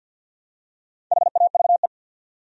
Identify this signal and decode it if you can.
Morse code, heard as a keyed tone, about 53 wpm, 700 Hz. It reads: HAVE
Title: Top 100 Words 50wpm